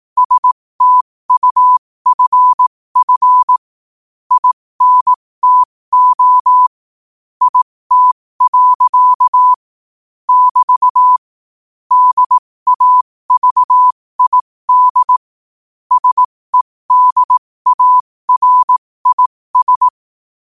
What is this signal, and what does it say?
Morse code, heard as a keyed tone, about 18 wpm, spaced as for 15 wpm, 1000 Hz. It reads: STUFF INTO IT. <BT> DAVID SEDARIS
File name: qotdfiles / 15wpm